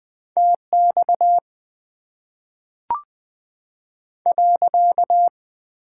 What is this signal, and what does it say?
Morse code, heard as a keyed tone, about 20 wpm, 700 Hz. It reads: T X .
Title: Top 90 QSO Elements - 8x Character Spacing - 20wpm